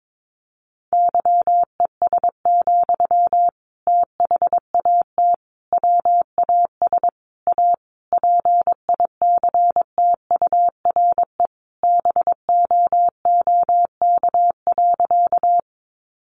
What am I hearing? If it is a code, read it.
Morse code, heard as a keyed tone, about 22 wpm, 700 Hz. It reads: YES, THAT WAS A PICTURE BOOK.
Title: Sentences from Top 500 Words 22wpm